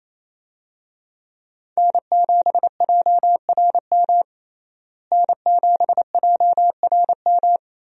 Morse code, heard as a keyed tone, about 28 wpm, 700 Hz. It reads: N7JRM N7JRM
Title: USA Call Signs - 5 Characters - Repeated 2x - 28wpm